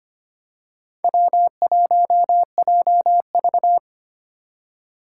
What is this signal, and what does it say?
Morse code, heard as a keyed tone, about 25 wpm, 700 Hz. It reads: W1JV
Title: USA Call Signs - 4 Characters - 2x Character Spacing - 25wpm